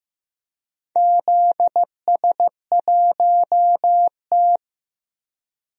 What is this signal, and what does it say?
Morse code, heard as a keyed tone, about 16 wpm, 700 Hz. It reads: ZS1T
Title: DXCC-100 Call Signs - 4 Characters - Repeated 3x - 15wpm